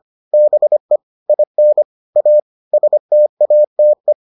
Morse code, heard as a keyed tone, about 25 wpm, 600 Hz. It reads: BE IN A STATE